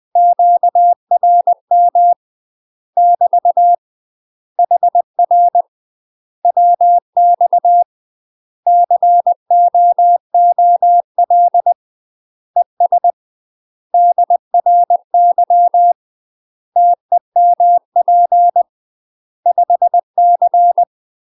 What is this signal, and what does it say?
Morse code, heard as a keyed tone, about 20 wpm, 700 Hz. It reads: QRM <BT> HR WX COOL ES DRY TEMP 5C
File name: UK MASTER QSOs TRAINING / 20WPM / UK MASTER QSOs 0024